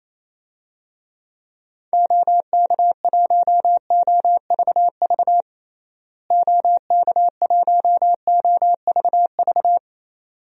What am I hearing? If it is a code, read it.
Morse code, heard as a keyed tone, about 28 wpm, 700 Hz. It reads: OK1OVV OK1OVV
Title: DXCC-100 Call Signs - 6 Characters - Repeated 2x - 28wpm